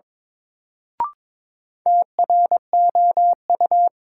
Morse code, heard as a keyed tone, about 21 wpm, 700 Hz. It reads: TROU
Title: Sentences from Top 1200 Words - New Word Drill Alt - 22wpm